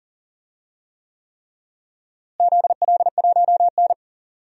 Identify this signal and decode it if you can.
Morse code, heard as a keyed tone, about 40 wpm, 700 Hz. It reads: ZL1N
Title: DXCC-100 Call Signs - 4 Characters - Repeated 3x - 40wpm